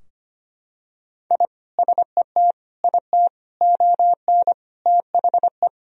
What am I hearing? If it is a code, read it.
Morse code, heard as a keyed tone, about 25 wpm, 700 Hz. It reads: I SET IT ON THE
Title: Sentences from Top 1000 Words - Review Part 2 - 25wpm